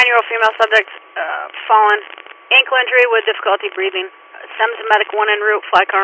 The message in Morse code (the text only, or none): none